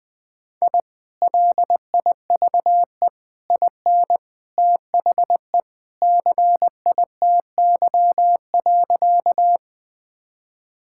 Morse code, heard as a keyed tone, about 20 wpm, 700 Hz. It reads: I LIVE IN THE CITY.
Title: Sentences with 2-Word Phrases - Spoken Beginning - 20wpm